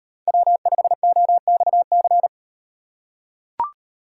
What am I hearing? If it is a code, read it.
Morse code, heard as a keyed tone, about 38 wpm, 700 Hz. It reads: W5OXC